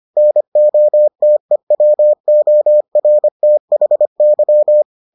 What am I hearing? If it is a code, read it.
Morse code, heard as a keyed tone, about 25 wpm, 600 Hz. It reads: NOTEWORTHY